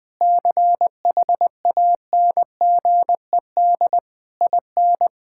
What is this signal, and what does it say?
Morse code, heard as a keyed tone, about 20 wpm, 700 Hz. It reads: CHANGED IN
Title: Sentences with 6-Word Phrases - Spoken Beginning - 20wpm